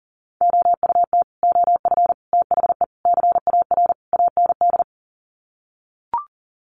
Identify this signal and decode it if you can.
Morse code, heard as a keyed tone, about 40 wpm, 700 Hz. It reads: OUT OF THE CAR AND
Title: Sets of 5 Words Encore 40wpm